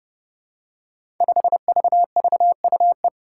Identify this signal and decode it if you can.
Morse code, heard as a keyed tone, about 32 wpm, 700 Hz. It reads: 5VVUE
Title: Five Challenging Letter-Number Combinations 30wpm